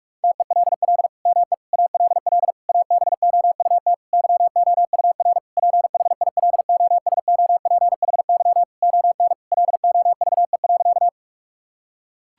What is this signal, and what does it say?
Morse code, heard as a keyed tone, about 46 wpm, 700 Hz. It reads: TELL ME ALL ABOUT YOUR PHILOSOPHY ON LOVE.
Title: Sentences from Top 1000 Words - Review Part 6 - 45wpm